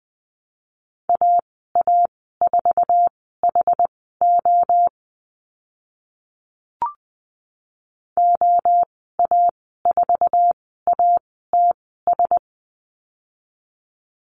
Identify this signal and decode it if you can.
Morse code, heard as a keyed tone, about 20 wpm, spaced as for 14 wpm, 700 Hz. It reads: AA4HO OA4ATS
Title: 082 Lesson 14g - Number 2 - Call Signs 20wpm